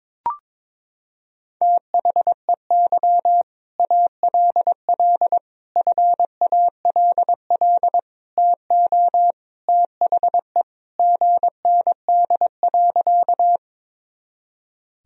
Morse code, heard as a keyed tone, about 22 wpm, 700 Hz. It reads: THEY ALL FALL TO THE GND.
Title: Sentences from Top 500 Words and Abbreviations 22wpm